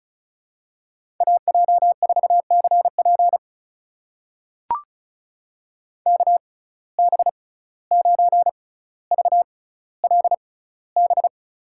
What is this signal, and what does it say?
Morse code, heard as a keyed tone, about 36 wpm, spaced as for 17 wpm, 700 Hz. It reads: AJ4CP KB9VLB
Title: USA Call Signs - 6x Character Spacing - 35wpm